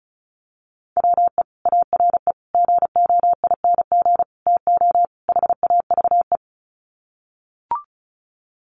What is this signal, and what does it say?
Morse code, heard as a keyed tone, about 34 wpm, 700 Hz. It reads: WE ARE GOING TO HAVE
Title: Sets of 5 Words Encore 35wpm